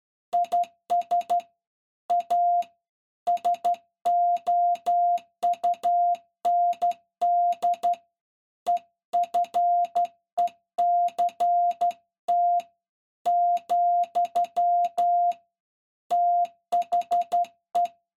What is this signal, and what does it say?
Morse code, heard as a keyed tone, about 12 wpm, 700 Hz. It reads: IS A SOUND EFECT , THE
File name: se / pre2 / Morse Code01-4(Slow-Key Noise)